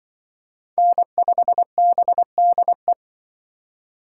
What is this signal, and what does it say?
Morse code, heard as a keyed tone, about 23 wpm, 700 Hz. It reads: N5BDE
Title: Speed Racing - USA 1x3 Call Signs 20wpm